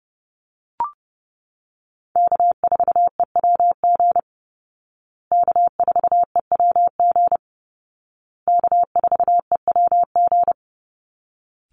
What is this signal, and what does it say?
Morse code, heard as a keyed tone, about 30 wpm, 700 Hz. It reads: K4EWG K4EWG K4EWG
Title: Call Signs Repeated 2x - 30wpm